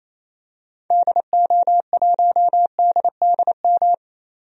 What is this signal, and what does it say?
Morse code, heard as a keyed tone, about 28 wpm, 700 Hz. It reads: DO1DDM
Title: DXCC-100 Call Signs - 6 Characters - 28wpm